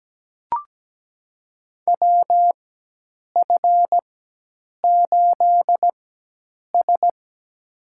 Morse code, heard as a keyed tone, about 17 wpm, spaced as for 8 wpm, 700 Hz. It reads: WF8S
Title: USA Call Signs - 4 Characters - 4x Character Spacing - 17wpm